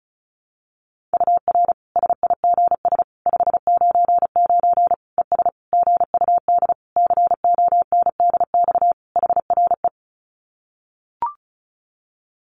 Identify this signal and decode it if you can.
Morse code, heard as a keyed tone, about 34 wpm, 700 Hz. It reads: UR SIGS 599 ES GUD CONDX HRE
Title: Common QSO Phrases 35wpm